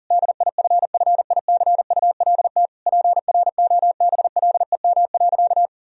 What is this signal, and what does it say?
Morse code, heard as a keyed tone, about 40 wpm, 700 Hz. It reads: DIFFICULT PROBLEM.
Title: Sentences from Top 1000 Words - Review Part 2 - 40wpm